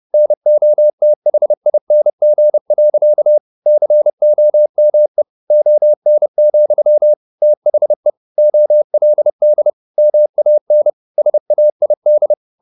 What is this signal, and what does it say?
Morse code, heard as a keyed tone, about 30 wpm, 600 Hz. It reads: NOTHING. COME ON, THE OLD MAN SAID
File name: the old man and the sea / the old man and the sea 1 30wpm